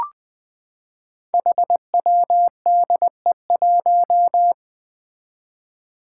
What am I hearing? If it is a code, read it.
Morse code, heard as a keyed tone, about 20 wpm, 700 Hz. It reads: HWDE1
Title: Five Challenging Letter-Number Combinations 20wpm